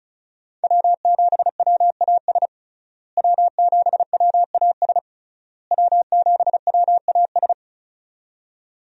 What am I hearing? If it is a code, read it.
Morse code, heard as a keyed tone, about 36 wpm, 700 Hz. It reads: W7WAS W7WAS W7WAS
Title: USA Call Signs - 5 Characters - Repeated 3x - 35wpm